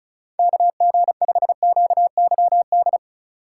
Morse code, heard as a keyed tone, about 34 wpm, 700 Hz. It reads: KG5QYD
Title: USA Call Signs - 6 Characters - 4x Character Spacing - 35wpm